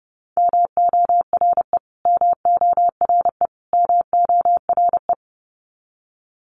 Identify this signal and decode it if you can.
Morse code, heard as a keyed tone, about 30 wpm, 700 Hz. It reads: MORE MORE MORE
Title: Top 200 Words Repeated 30wpm